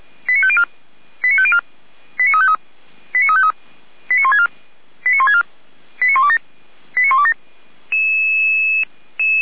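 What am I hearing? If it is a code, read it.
Five-tone selective call (ZVEI), heard as quick runs of five tones, tones of about 70 ms. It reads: 89504 89504 89353 89353 89164 89164 89127 89127
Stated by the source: ZVEI five-tone call received by SDR, feuerwehr-orlamuende.org fire brigade audio archive, code 89250.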